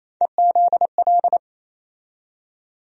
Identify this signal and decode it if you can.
Morse code, heard as a keyed tone, about 28 wpm, 700 Hz. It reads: EZL
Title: USA Call Sign Postfixes - 3 Letters 28wpm